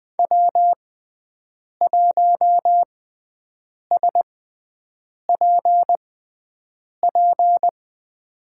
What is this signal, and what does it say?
Morse code, heard as a keyed tone, about 20 wpm, spaced as for 7 wpm, 700 Hz. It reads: W1SPP
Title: USA Call Signs - 5 Characters - 6x Character Spacing - 20wpm